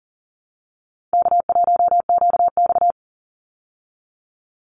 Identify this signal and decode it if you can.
Morse code, heard as a keyed tone, about 40 wpm, 700 Hz. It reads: K1QX
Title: Call Signs Repeated 2x - 40wpm